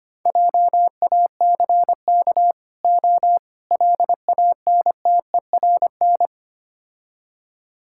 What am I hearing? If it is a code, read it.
Morse code, heard as a keyed tone, about 27 wpm, 700 Hz. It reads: JACK O LANTERN
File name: Easy Words - Speed-Racing with Hints - 25wpm